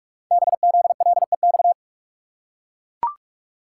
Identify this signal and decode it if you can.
Morse code, heard as a keyed tone, about 44 wpm, 700 Hz. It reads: BZLEX
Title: Code Groups - 5 Letters - 45wpm